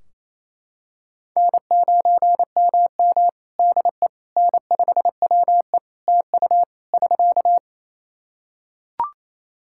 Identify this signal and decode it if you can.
Morse code, heard as a keyed tone, about 28 wpm, 700 Hz. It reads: N9MM DE N5WE TU <SK>
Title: Practice QSO - Exchange 4 - Ending - 28wpm